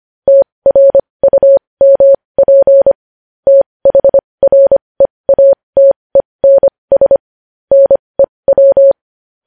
Morse code, heard as a keyed tone, about 25 wpm, spaced as for 20 wpm, 550 Hz. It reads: TRUMP THREATENS NEW
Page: news headlines in morse code at 20 wpm